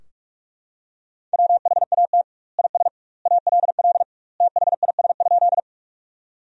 Morse code, heard as a keyed tone, about 46 wpm, 700 Hz. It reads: WHAT IS ALL THIS?
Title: 2x Word-Spacing - Sentences from Top 100 Words - 45wpm